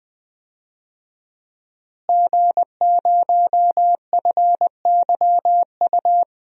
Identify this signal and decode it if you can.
Morse code, heard as a keyed tone, about 21 wpm, 700 Hz. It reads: G0FYU
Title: DXCC-100 Call Signs - 5 Characters - Repeated 2x - 20wpm